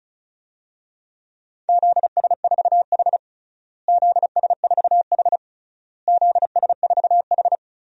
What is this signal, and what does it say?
Morse code, heard as a keyed tone, about 34 wpm, 700 Hz. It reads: ZS4H ZS4H ZS4H
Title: DXCC-100 Call Signs - 4 Characters - Repeated 3x - 35wpm